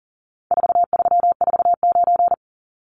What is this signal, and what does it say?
Morse code, heard as a keyed tone, about 40 wpm, 700 Hz. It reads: 4349
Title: Numbers - 4-Digit 40wpm